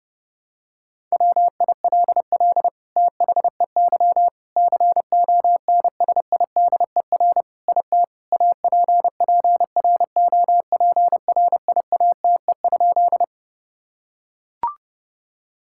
Morse code, heard as a keyed tone, about 30 wpm, 700 Hz. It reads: WILL THEY CONSIDER IT APPROPRIATE?
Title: Sentences from Top 1000 Words - Review Part 5 - 30wpm